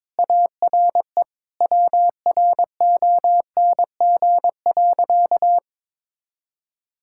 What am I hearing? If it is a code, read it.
Morse code, heard as a keyed tone, about 21 wpm, 700 Hz. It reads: ARE WRONG.
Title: Sentences from Top 1000 Words - Review Part 5 - 22wpm